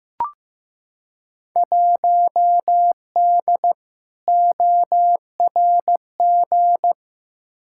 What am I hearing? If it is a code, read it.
Morse code, heard as a keyed tone, about 15 wpm, 700 Hz. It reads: 1D ORG
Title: Field Day Exchanges - 15wpm